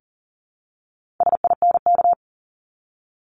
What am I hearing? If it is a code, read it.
Morse code, heard as a keyed tone, about 40 wpm, 700 Hz. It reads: SINK
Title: Multi-letter Phonemes Repeated 3x 40wpm